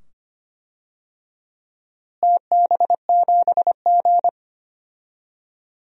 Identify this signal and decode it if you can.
Morse code, heard as a keyed tone, about 25 wpm, 700 Hz. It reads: TB7G
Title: DXCC-100 Call Signs - 4 Characters - Repeated 2x - 25wpm